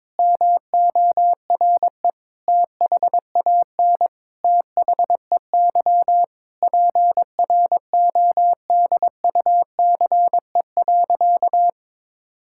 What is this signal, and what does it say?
Morse code, heard as a keyed tone, about 22 wpm, 700 Hz. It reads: MORE THAN THEY PRODUCE.
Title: Sentences from Top 1000 Words - Review Part 5 - 22wpm